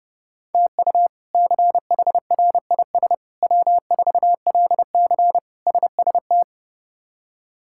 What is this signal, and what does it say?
Morse code, heard as a keyed tone, about 31 wpm, 700 Hz. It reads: TU CHRIS W4LC SST
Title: K1USN SST QSO Practice - 30wpm